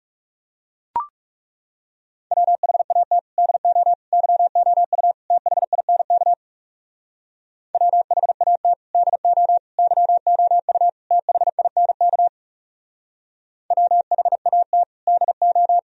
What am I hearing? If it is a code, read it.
Morse code, heard as a keyed tone, about 37 wpm, 700 Hz. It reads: WHAT DO YOU THINK WHAT DO YOU THINK WHAT DO
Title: Speed-Racing - Sets of 4 Words 30wpm